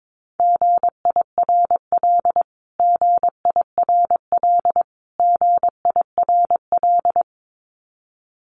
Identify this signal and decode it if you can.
Morse code, heard as a keyed tone, about 22 wpm, 700 Hz. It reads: GIRL GIRL GIRL
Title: Top 500 Words Repeated 22wpm